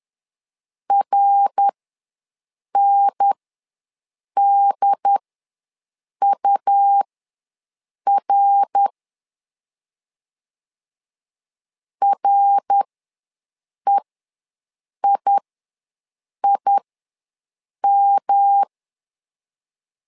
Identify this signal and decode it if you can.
Morse code, heard as a keyed tone, about 11 wpm, spaced as for 6 wpm, 800 Hz. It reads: RNDUR REIIM
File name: LEZIONI / 3 MWRO / 3 MWR ws 10wpm 2151